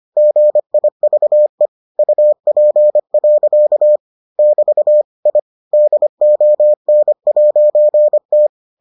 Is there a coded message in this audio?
Yes: Morse code from 0.2 to 8.5 s, heard as a keyed tone, about 25 wpm, 600 Hz. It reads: GIVE UP. <BT> I DON'T